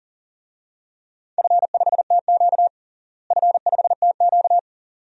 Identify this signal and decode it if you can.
Morse code, heard as a keyed tone, about 40 wpm, 700 Hz. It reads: F5TQ F5TQ
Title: DXCC-100 Call Signs - 4 Characters - Repeated 2x - 40wpm